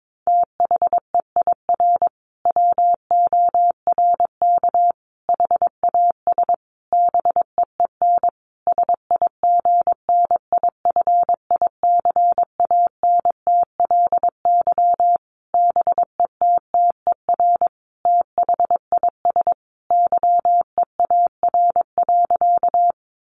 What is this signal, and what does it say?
Morse code, heard as a keyed tone, about 22 wpm, 700 Hz. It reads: THEIR WORK HAS BEEN SIGNIFICANTLY BETTER THIS YEAR.
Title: Sentences from Top 1500 Words 22wpm